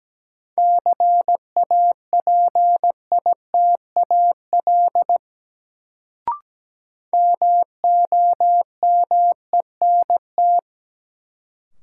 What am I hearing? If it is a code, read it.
Morse code, heard as a keyed tone, about 17 wpm, 700 Hz. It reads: CAPITAL MOMENT
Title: Sentences from Top 1000 Words - New Word Drill - 17wpm